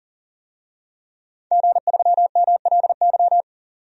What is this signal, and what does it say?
Morse code, heard as a keyed tone, about 40 wpm, 700 Hz. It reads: G3MLY
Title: DXCC-100 Call Signs - 5 Characters - 40wpm